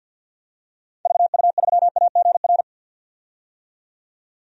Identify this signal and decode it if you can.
Morse code, heard as a keyed tone, about 51 wpm, 700 Hz. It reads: VU3AGR
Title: DXCC-100 Call Signs - 6 Characters - 50wpm